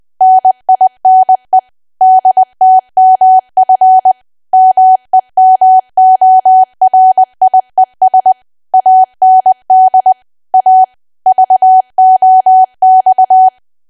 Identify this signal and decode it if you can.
Morse code, heard as a keyed tone, about 20 wpm, 750 Hz. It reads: NINE DTMF MEMORIES AND A VOX
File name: Archive / 20 WPM / 20WPM